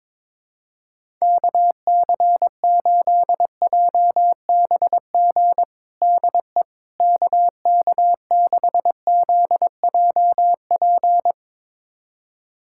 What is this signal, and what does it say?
Morse code, heard as a keyed tone, about 22 wpm, 700 Hz. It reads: KC8JBG DE KK6ZJP
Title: USA Callsign DE Callsign - 6 Characters - 22wpm